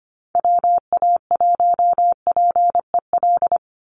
Morse code, heard as a keyed tone, about 25 wpm, 700 Hz. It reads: WA1PEL
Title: USA Call Signs 25wpm